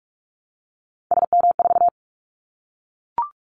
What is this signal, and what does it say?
Morse code, heard as a keyed tone, about 44 wpm, 700 Hz. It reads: SM4
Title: Callsign Prefixes for Top 25 DXCC - Repeated 3x - 45wpm